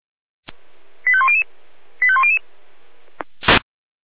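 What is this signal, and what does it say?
Five-tone selective call (ZVEI), heard as quick runs of five tones, tones of about 70 ms. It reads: 85100 85100
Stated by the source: ZVEI five-tone selective call 85839 received by SDR.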